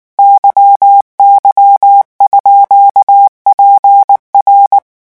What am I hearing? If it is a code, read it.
Morse code, heard as a keyed tone, about 19 wpm, 800 Hz. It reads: YY_PR
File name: morse mp3 / revision14-vitesse 18 mots